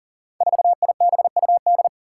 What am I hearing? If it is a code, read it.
Morse code, heard as a keyed tone, about 37 wpm, 700 Hz. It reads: 4IBUD
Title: Five Challenging Letter-Number Combinations 40wpm